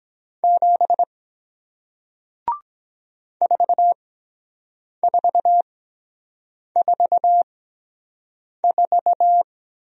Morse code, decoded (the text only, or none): 7 4444